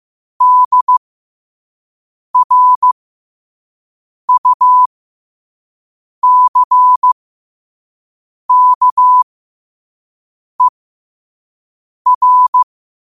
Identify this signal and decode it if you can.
Morse code, heard as a keyed tone, about 15 wpm, spaced as for 5 wpm, 1000 Hz. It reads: DRUCKER